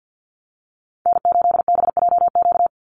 Morse code, heard as a keyed tone, about 50 wpm, 700 Hz. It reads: N8BJQ
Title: Call Signs - 50wpm